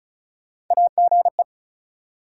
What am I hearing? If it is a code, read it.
Morse code, heard as a keyed tone, about 34 wpm, 700 Hz. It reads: AGE
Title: Sets of 3 Words - New Word Drill - 35wpm